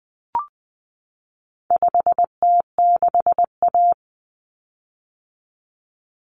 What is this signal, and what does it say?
Morse code, heard as a keyed tone, about 20 wpm, 700 Hz. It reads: 5T6A
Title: Common Cut Numbers - 4-Digit 20wpm